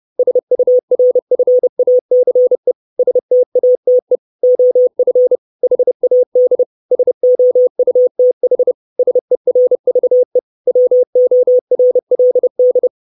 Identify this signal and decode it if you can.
Morse code, heard as a keyed tone, about 30 wpm, 500 Hz. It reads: SURFACE STATE OF HAD SOUTH SERVE WORLD
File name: 632 Words 30WPM